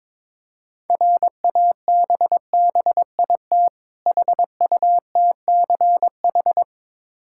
Morse code, heard as a keyed tone, about 22 wpm, 690 Hz. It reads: RABBIT HUTCH
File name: Difficult Words with Hints - 22wpm